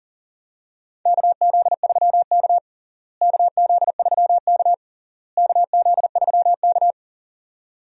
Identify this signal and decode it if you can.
Morse code, heard as a keyed tone, about 40 wpm, 700 Hz. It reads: KZ3K KZ3K KZ3K
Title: USA Call Signs - 4 Characters - Repeated 3x - 40wpm